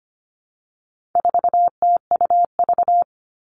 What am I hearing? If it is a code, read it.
Morse code, heard as a keyed tone, about 25 wpm, 700 Hz. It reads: 4TUV
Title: All Cut Numbers - 4-Digit 25wpm